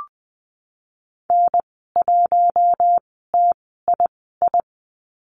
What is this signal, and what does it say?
Morse code, heard as a keyed tone, about 20 wpm, spaced as for 15 wpm, 700 Hz. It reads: N1TII
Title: 076 Lesson 13g - Letter L - Call Signs 20wpm